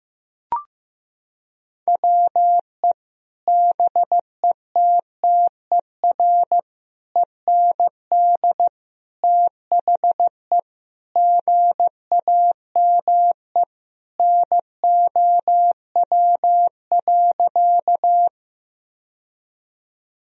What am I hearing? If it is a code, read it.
Morse code, heard as a keyed tone, about 15 wpm, 700 Hz. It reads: WE BETTER END THE GAME NOW.
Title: Sentences from Top 1000 Words - Review Part 3 - 15wpm